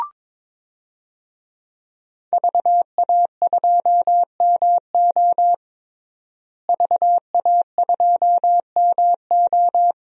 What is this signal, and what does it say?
Morse code, heard as a keyed tone, about 21 wpm, 700 Hz. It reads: VA2MO VA2MO
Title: DXCC-100 Call Signs - 5 Characters - Repeated 3x - 22wpm